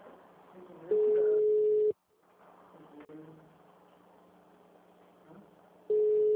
Ringback tone (Europe, 425 Hz).